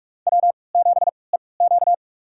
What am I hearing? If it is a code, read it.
Morse code, heard as a keyed tone, about 46 wpm, spaced as for 26 wpm, 700 Hz. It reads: W7EQ
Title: USA Call Signs - 3x Character Spacing - 45wpm